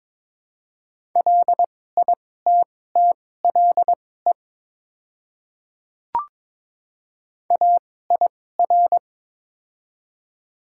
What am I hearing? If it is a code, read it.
Morse code, heard as a keyed tone, about 23 wpm, spaced as for 16 wpm, 700 Hz. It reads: LITTLE AIR
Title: 2x Character Spacing - Top 500 Words - 22wpm